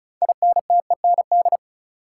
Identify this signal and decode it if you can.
Morse code, heard as a keyed tone, about 36 wpm, 700 Hz. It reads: INTEND
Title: Sentences from Top 1600 Words - New Word Drill Alt - 35wpm